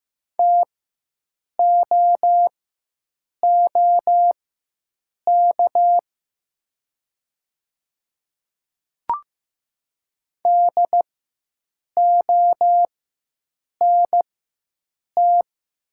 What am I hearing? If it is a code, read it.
Morse code, heard as a keyed tone, about 15 wpm, spaced as for 7 wpm, 700 Hz. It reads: TOOK DONT
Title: 4x Character Spacing - Top 500 Words - 15wpm